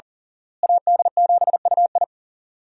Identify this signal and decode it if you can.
Morse code, heard as a keyed tone, about 40 wpm, 700 Hz. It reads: AD7UI
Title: USA Call Signs - 5 Characters - Repeated 3x - 40wpm